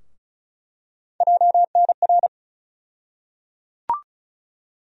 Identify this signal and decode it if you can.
Morse code, heard as a keyed tone, about 34 wpm, 700 Hz. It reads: JNR
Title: USA Call Sign Postfixes - 3 Letters 35wpm